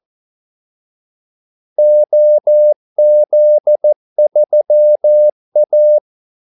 Morse code, heard as a keyed tone, about 14 wpm, 600 Hz. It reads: OZ3A
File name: Callsigns-14wpm